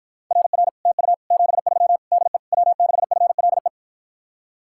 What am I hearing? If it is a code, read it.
Morse code, heard as a keyed tone, about 52 wpm, 700 Hz. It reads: RR TU 73 DE W6FLE